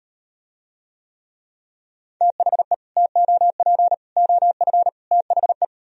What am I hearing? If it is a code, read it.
Morse code, heard as a keyed tone, about 38 wpm, 700 Hz. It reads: THE TOP OF THE